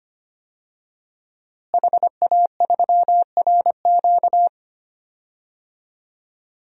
Morse code, heard as a keyed tone, about 25 wpm, 700 Hz. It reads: HA3RQ
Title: DXCC-100 Call Signs - 5 Characters - Repeated 2x - 25wpm